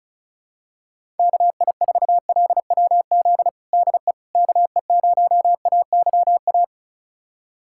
Morse code, heard as a keyed tone, about 34 wpm, 700 Hz. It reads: KI4LWZ DE KE0AYA
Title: USA Callsign DE Callsign - 6 Characters - 35wpm